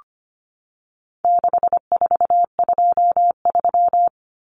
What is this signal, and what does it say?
Morse code, heard as a keyed tone, about 25 wpm, 700 Hz. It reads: 6423
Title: Numbers - 4-Digit 25wpm